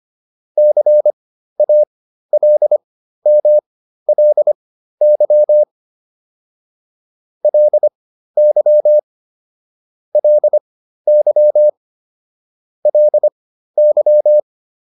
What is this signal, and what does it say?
Morse code, heard as a keyed tone, about 25 wpm, spaced as for 13 wpm, 600 Hz. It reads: CALMLY LY LY LY